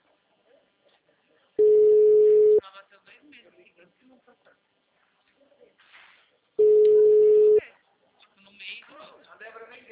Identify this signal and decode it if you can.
Ringback tone (Europe, 425 Hz), with speech.